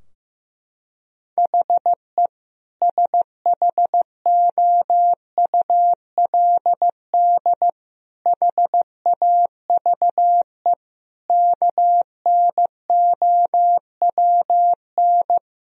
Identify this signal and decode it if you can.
Morse code, heard as a keyed tone, about 16 wpm, 700 Hz. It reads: HE SHOULD HAVE KNOWN
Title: Sentences from Top 1000 Words - Review Part 3 - 15wpm